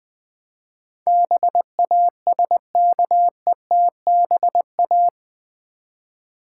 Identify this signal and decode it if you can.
Morse code, heard as a keyed tone, about 20 wpm, 700 Hz. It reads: BASKETBA
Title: Build A Word - 4 Letters At A Time - 10-Letter Words - 20wpm